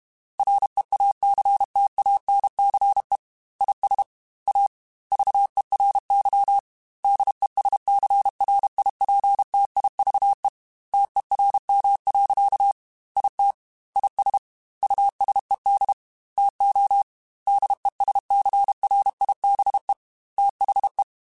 Morse code, heard as a keyed tone, about 32 wpm, 800 Hz. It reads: REACTANCE IS A VERY DESCRIPTIVE TERM. IT IS USED TO DESCRIBE THE
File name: Technical Subjects / Technical Subjects-mp3s / Reactance @30